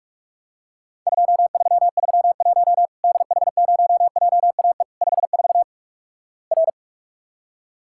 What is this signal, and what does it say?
Morse code, heard as a keyed tone, about 46 wpm, 700 Hz. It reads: 2331 DH0JAE 54 R
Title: Worked All Europe DX Contest - QTC Exchange - 45wpm